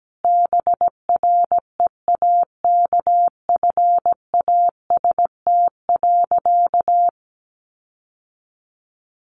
Morse code, heard as a keyed tone, about 17 wpm, 700 Hz. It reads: BREAKFAST.
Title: Sentences from Top 1500 Words 17wpm